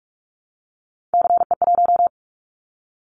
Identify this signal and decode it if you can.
Morse code, heard as a keyed tone, about 44 wpm, 700 Hz. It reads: CE1
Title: Callsign Prefixes for Top 100 DXCC - 45wpm